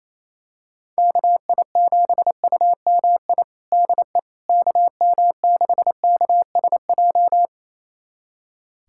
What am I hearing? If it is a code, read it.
Morse code, heard as a keyed tone, about 28 wpm, 700 Hz. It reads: KI7UMI DE KM6KSJ
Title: USA Callsign DE Callsign - 6 Characters - 28wpm